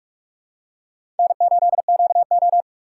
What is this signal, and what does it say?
Morse code, heard as a keyed tone, about 46 wpm, 700 Hz. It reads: N8QO
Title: USA Call Signs - 4 Characters - 4x Character Spacing - 45wpm